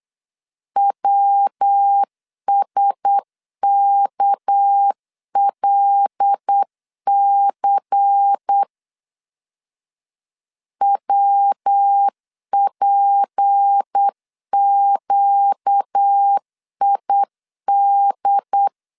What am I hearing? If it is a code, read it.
Morse code, heard as a keyed tone, about 8 wpm, 800 Hz. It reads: WSKLC WPQID
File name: LEZIONI / 6 PKX0 / 6 PK 8wpm 3003